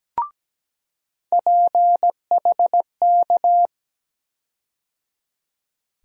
Morse code, heard as a keyed tone, about 17 wpm, 700 Hz. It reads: PHK
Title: USA Call Sign Postfixes - 3 Letters 17wpm